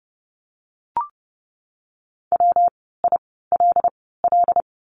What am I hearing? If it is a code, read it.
Morse code, heard as a keyed tone, about 30 wpm, spaced as for 17 wpm, 700 Hz. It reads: WILL
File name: course / 30:10wpm / 115 Lesson 19f - Letter w - Words 30wpm